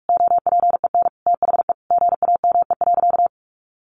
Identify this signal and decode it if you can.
Morse code, heard as a keyed tone, about 44 wpm, 700 Hz. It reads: OPEN THE GAME.
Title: Sentences from Top 1500 Words 45wpm